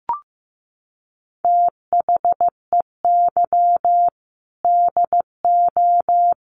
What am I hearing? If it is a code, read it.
Morse code, heard as a keyed tone, about 15 wpm, 700 Hz. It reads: THEY DO
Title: Sets of 5 Words 15wpm